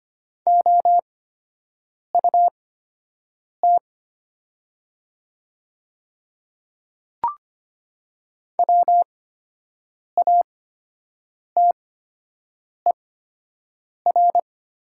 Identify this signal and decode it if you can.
Morse code, heard as a keyed tone, about 25 wpm, spaced as for 7 wpm, 700 Hz. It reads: OUT WATER
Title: 8x Character Spacing - Top 100 Words - 25wpm